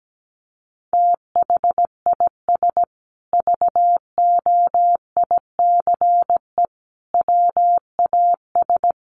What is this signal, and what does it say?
Morse code, heard as a keyed tone, about 17 wpm, 700 Hz. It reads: THIS VOICE WAS
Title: Sentences from Top 500 Words 17wpm